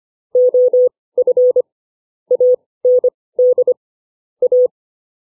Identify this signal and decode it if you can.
Morse code, heard as a keyed tone, about 25 wpm, spaced as for 18 wpm, 500 Hz. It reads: OF AND A
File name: CW / effective wpm